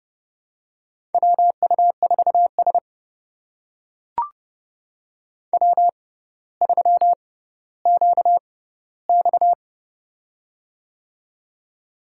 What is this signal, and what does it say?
Morse code, heard as a keyed tone, about 30 wpm, spaced as for 15 wpm, 700 Hz. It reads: WU4S W3QX
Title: USA Call Signs - 6x Character Spacing - 30wpm